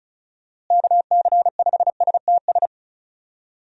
Morse code, heard as a keyed tone, about 36 wpm, 700 Hz. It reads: KC5STS
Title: USA Call Signs - 6 Characters - Repeated 3x - 35wpm